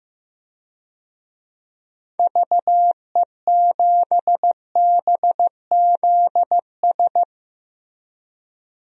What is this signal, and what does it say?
Morse code, heard as a keyed tone, about 15 wpm, 700 Hz. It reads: VE7BZS
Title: DXCC-100 Call Signs - 6 Characters - Repeated 2x - 15wpm